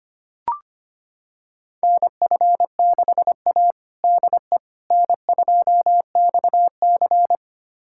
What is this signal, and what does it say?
Morse code, heard as a keyed tone, about 25 wpm, 700 Hz. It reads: NF6A DE N2XC
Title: USA Callsign DE Callsign - 4 Characters - 25wpm